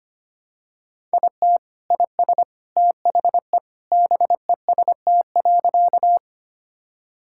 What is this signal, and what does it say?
Morse code, heard as a keyed tone, about 25 wpm, 700 Hz. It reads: IT IS THE BEST.
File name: practice / 25wpm / Sentences with 3-Word Phrases - 25wpm